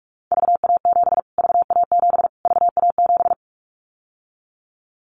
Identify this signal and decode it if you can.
Morse code, heard as a keyed tone, about 44 wpm, 700 Hz. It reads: VA7 VA7 VA7
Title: Callsign Prefixes for Top 25 DXCC - Repeated 3x - 45wpm